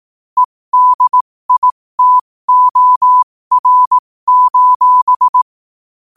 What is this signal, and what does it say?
Morse code, heard as a keyed tone, about 18 wpm, spaced as for 15 wpm, 1000 Hz. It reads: EDITOR:
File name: qotdfiles / 15wpm